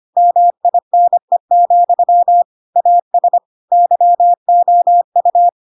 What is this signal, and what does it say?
Morse code, heard as a keyed tone, about 25 wpm, 700 Hz. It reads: MINE, AS YOU
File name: Pinocchio / Pinocchio ch4 25 25 ik1hge